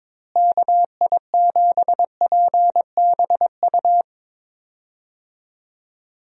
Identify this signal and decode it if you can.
Morse code, heard as a keyed tone, about 22 wpm, 700 Hz. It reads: KI7PBU
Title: USA Call Signs - 6 Characters - Repeated 3x - 22wpm